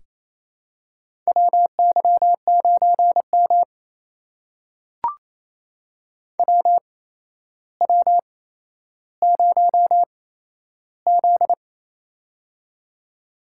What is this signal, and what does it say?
Morse code, heard as a keyed tone, about 28 wpm, spaced as for 12 wpm, 700 Hz. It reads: WY9M WW0Z
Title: USA Call Signs - 4 Characters - 8x Character Spacing - 28wpm